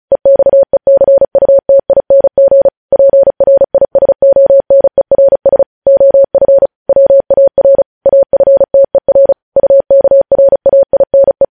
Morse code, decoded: EXECUTING PRISONERS OF WAR AFTER UKRAINE

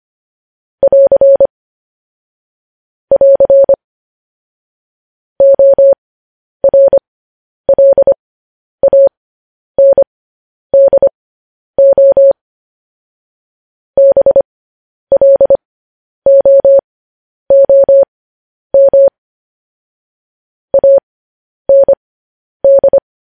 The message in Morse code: <AR> <AR> ORLANDO BLOOM AND

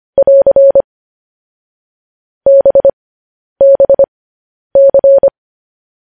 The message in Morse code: <AR> BBC

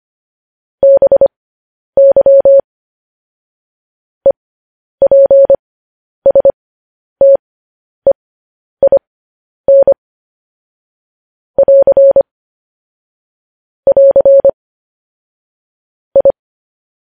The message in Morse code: BY EPSTEIN <AR> <AR> I

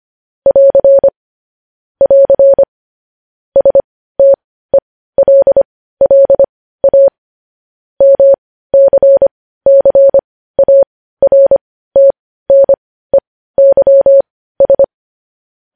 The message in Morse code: <AR> <AR> STELLA MCCARTNEYS

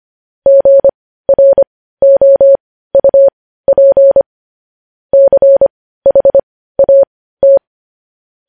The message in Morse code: GROUP CHAT

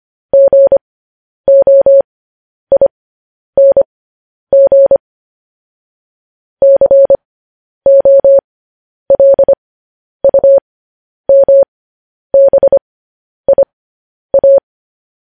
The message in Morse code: GOING COLUMBIA

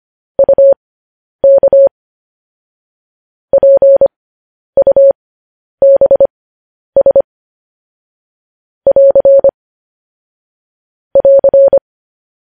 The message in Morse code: UK PUBS <AR> <AR>

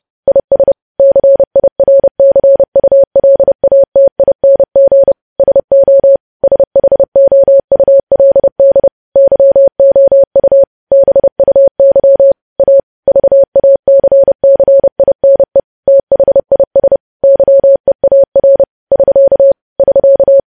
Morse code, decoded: IS CIRCULATING SO SHOULD YOU BUY A VACCINE THIS YEAR <SK> <SK>

